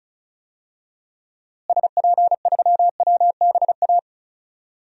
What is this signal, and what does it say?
Morse code, heard as a keyed tone, about 34 wpm, 700 Hz. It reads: SP3WBA
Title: DXCC-100 Call Signs - 6 Characters - Repeated 2x - 35wpm